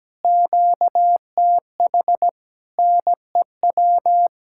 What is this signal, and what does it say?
Morse code, heard as a keyed tone, about 17 wpm, 700 Hz. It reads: QTH NEW
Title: Speed-Racing Common QSO Phrases - 17wpm